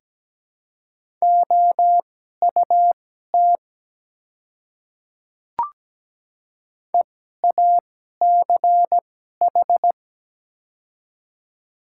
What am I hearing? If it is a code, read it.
Morse code, heard as a keyed tone, about 17 wpm, spaced as for 12 wpm, 690 Hz. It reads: OUT EACH
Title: 2x Character Spacing - Top 100 Words - 17wpm